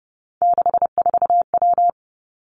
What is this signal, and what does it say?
Morse code, heard as a keyed tone, about 31 wpm, 700 Hz. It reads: 64W
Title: Three Challenging Letter-Number Combinations 30wpm